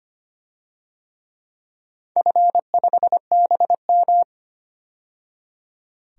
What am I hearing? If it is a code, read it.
Morse code, heard as a keyed tone, about 25 wpm, 700 Hz. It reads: F5BM
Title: DXCC-100 Call Signs - 4 Characters - 25wpm